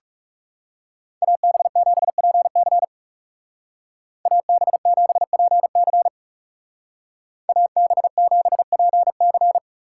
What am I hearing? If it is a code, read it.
Morse code, heard as a keyed tone, about 36 wpm, 700 Hz. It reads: AB7PC AB7PC AB7PC
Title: Speed Racing - USA 2x2 Call Signs 30wpm